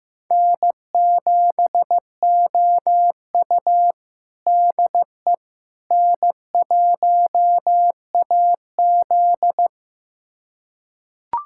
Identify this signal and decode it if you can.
Morse code, heard as a keyed tone, about 15 wpm, 700 Hz. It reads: N7OU DE N1AZ
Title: USA Callsign DE Callsign - 4 Characters - 15wpm